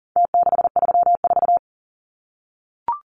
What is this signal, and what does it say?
Morse code, heard as a keyed tone, about 37 wpm, 700 Hz. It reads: T634
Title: Numbers - Contest Style - 4-Digit 40wpm